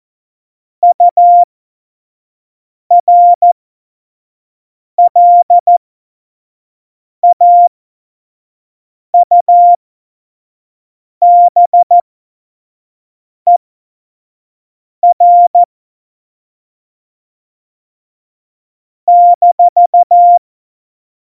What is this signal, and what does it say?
Morse code, heard as a keyed tone, about 14 wpm, spaced as for 5 wpm, 700 Hz. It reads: URLAUBER -
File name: Morsekurs / Witze / Witz 27. Oktober 2020, 25 F70